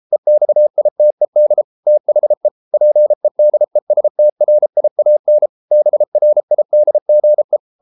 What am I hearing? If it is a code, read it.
Morse code, heard as a keyed tone, about 33 wpm, 600 Hz. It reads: EXITED THE PEDESTRIAN BRIDGE